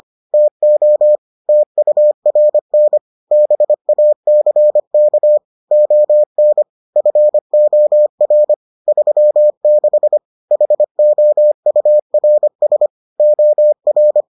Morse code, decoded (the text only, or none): TO TURN BACK ON FOR 36 HOURS OR